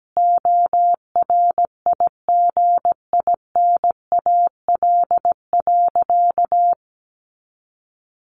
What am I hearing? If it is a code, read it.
Morse code, heard as a keyed tone, about 17 wpm, 700 Hz. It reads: ORIGINAL.
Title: Sentences from Top 1400 Words 17wpm